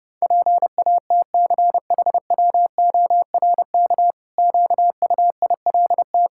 Morse code, heard as a keyed tone, about 30 wpm, 700 Hz. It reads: PATCHWORK QUILT
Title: Pangram Challenge v2 - 30wpm